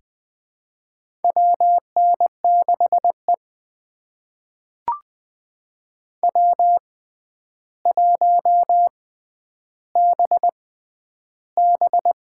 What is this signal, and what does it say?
Morse code, heard as a keyed tone, about 20 wpm, spaced as for 10 wpm, 700 Hz. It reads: WN6E W1BB
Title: USA Call Signs - 4 Characters - 6x Character Spacing - 20wpm